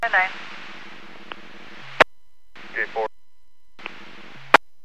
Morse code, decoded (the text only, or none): none